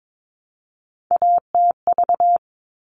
Morse code, heard as a keyed tone, about 22 wpm, 700 Hz. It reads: ATV
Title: All Cut Numbers - 3-Digit 22wpm